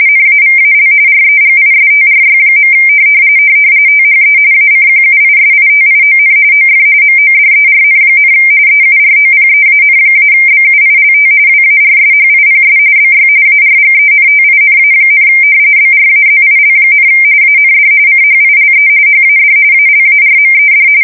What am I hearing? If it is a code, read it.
RTTY, about 45.45 baud. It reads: NAVAL REGISTRY. THE ACQUISITION OF MODERN, EUROPEAN-BUILT WARSHIPS BY ARGENTINA, BRAZIL, AND CHILE HAD ALARMED
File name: fsk170 / battleships